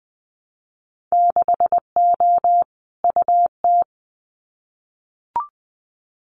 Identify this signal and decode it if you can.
Morse code, heard as a keyed tone, about 20 wpm, 700 Hz. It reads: 6O UT
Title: Winter Field Day Exchanges Encore 20wpm